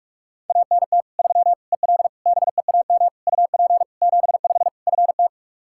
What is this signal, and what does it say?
Morse code, heard as a keyed tone, about 46 wpm, 700 Hz. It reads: ANT 3 EL BEAM UP 75 FT
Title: Practice QSO - Exchange 2 - Info - 45wpm